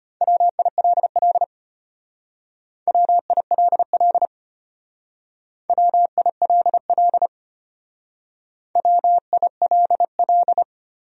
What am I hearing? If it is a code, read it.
Morse code, heard as a keyed tone, about 34 wpm, 700 Hz. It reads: WILL WILL WILL WILL